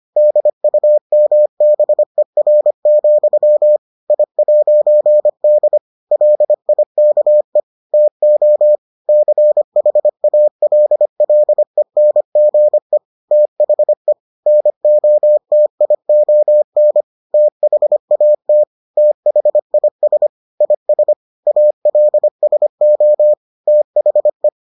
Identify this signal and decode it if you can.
Morse code, heard as a keyed tone, about 25 wpm, 600 Hz. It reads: DUMBER, I'D LIKE TO CHALLENGE THE NOTION THAT THIS IS ALSO THE